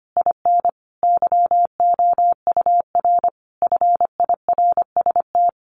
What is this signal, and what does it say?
Morse code, heard as a keyed tone, about 25 wpm, 700 Hz. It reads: IN YOUR FIRST
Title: Sentences from Top 1500 Words 25wpm